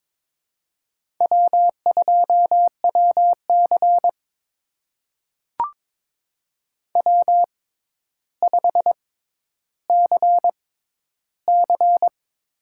Morse code, heard as a keyed tone, about 22 wpm, spaced as for 11 wpm, 700 Hz. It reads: W2WC W5CC
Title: USA Call Signs - 4 Characters - 6x Character Spacing - 22wpm